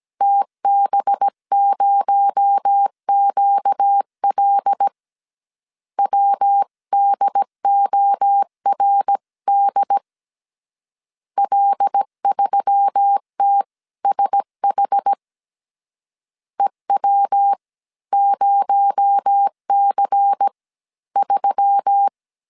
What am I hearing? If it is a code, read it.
Morse code, heard as a keyed tone, about 17 wpm, spaced as for 14 wpm, 800 Hz. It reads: TB0QL WDORD L3TSH EW0C3
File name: CW / DET / 1 6 1500C S A 16wpm 2635